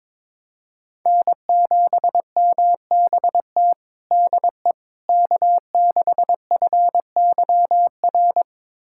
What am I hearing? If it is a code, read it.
Morse code, heard as a keyed tone, about 22 wpm, 700 Hz. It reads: N7MBT DE K6FYR
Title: USA Callsign DE Callsign - 5 Characters - 22wpm